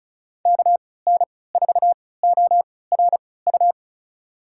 Morse code, heard as a keyed tone, about 36 wpm, spaced as for 20 wpm, 700 Hz. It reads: KN4ORU
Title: USA Call Signs - 6 Characters - 3x Character Spacing - 35wpm